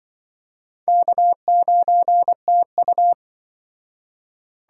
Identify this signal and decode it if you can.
Morse code, heard as a keyed tone, about 24 wpm, 700 Hz. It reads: K9TU
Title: Speed Racing - USA 1x2 Call Signs 20wpm